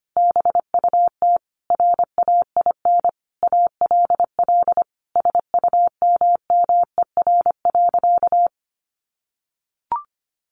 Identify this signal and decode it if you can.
Morse code, heard as a keyed tone, about 25 wpm, 700 Hz. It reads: BUT RAIN ALL SUMMER.
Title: Sentences from Top 1100 Words 25wpm